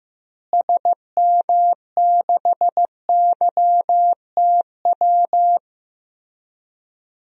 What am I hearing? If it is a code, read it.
Morse code, heard as a keyed tone, about 15 wpm, 700 Hz. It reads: SM6YTW
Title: DXCC-100 Call Signs - 6 Characters - Repeated 2x - 15wpm